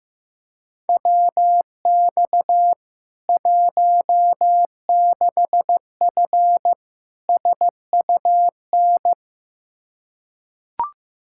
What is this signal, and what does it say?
Morse code, heard as a keyed tone, about 15 wpm, 700 Hz. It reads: WX 16F SUN
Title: K1USN SST QSO Practice 15wpm